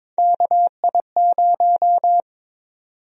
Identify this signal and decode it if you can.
Morse code, heard as a keyed tone, about 22 wpm, 700 Hz. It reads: KI0
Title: USA Call Sign Prefixes - 22wpm